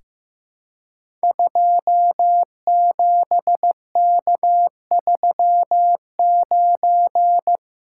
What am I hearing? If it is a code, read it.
Morse code, heard as a keyed tone, about 15 wpm, 700 Hz. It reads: 27K39
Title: Code Groups - 5 Letter:Numbers - 15wpm